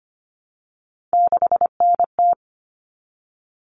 Morse code, heard as a keyed tone, about 25 wpm, 700 Hz. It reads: 6NT
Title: All Cut Numbers - 3-Digit 25wpm